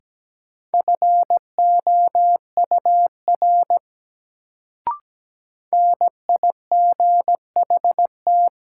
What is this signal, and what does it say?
Morse code, heard as a keyed tone, about 17 wpm, 700 Hz. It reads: FOUR NIGHT
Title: Sentences from Top 300 Words - New Word Drill - 17wpm